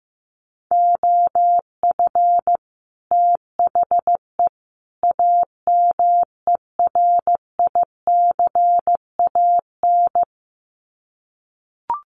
Morse code, heard as a keyed tone, about 15 wpm, 700 Hz. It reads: OF THE AMERICAN
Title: Sets of 3 Words 15wpm